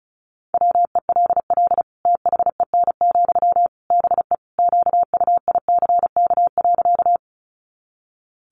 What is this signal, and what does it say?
Morse code, heard as a keyed tone, about 34 wpm, 700 Hz. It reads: WELL THEN, BE QUICK.
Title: Sentences from Top 500 Words 35wpm